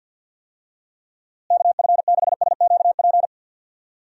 Morse code, heard as a keyed tone, about 52 wpm, 700 Hz. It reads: KF6SQP
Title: USA Call Signs - 6 Characters - Repeated 2x - 50wpm